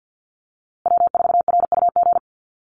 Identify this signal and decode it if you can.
Morse code, heard as a keyed tone, about 50 wpm, 700 Hz. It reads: W4RUG
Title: USA Call Signs Repeated 2x 50wpm